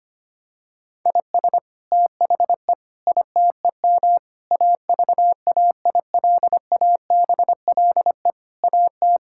Morse code, heard as a keyed tone, about 25 wpm, 700 Hz. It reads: IS THE ITEM AVAILABLE AT
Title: Sentences from Top 1000 Words - Review Part 6 - 25wpm